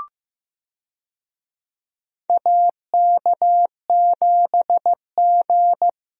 Morse code, heard as a keyed tone, about 15 wpm, 700 Hz. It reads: AK7G
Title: USA Call Signs - 4 Characters - Repeated 2x - 15wpm